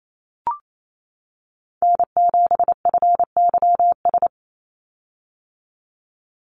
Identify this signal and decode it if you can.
Morse code, heard as a keyed tone, about 28 wpm, 700 Hz. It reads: N7FYS
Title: USA Call Signs 28wpm